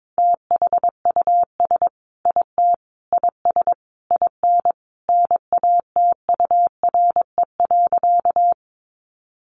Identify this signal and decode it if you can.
Morse code, heard as a keyed tone, about 22 wpm, 700 Hz. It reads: THUS IT IS IN NATURE.
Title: Sentences from Top 800 Words 22wpm